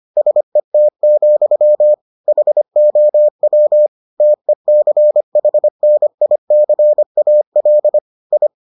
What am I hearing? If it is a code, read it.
Morse code, heard as a keyed tone, about 25 wpm, 600 Hz. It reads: SET, HOW TECHNICAL I